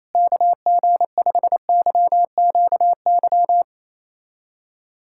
Morse code, heard as a keyed tone, about 28 wpm, 700 Hz. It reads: KG5YQY
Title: USA Callsign DE Callsign - 6 Characters - 28wpm